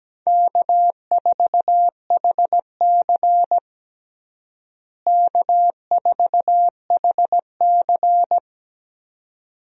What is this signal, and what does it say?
Morse code, heard as a keyed tone, about 17 wpm, 700 Hz. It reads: K4HC K4HC
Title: USA Call Signs - 4 Characters - Repeated 2x - 17wpm